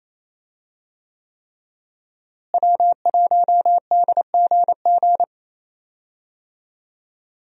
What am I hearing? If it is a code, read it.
Morse code, heard as a keyed tone, about 27 wpm, 700 Hz. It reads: W1DGG
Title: USA Call Signs - 5 Characters - Repeated 3x - 28wpm